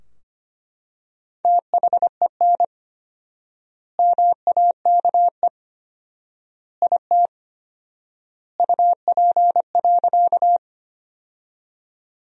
Morse code, heard as a keyed tone, about 25 wpm, 700 Hz. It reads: THEN MAKE IT UP.
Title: 4x Word-Spacing - Sentences from Top 100 Words - 25wpm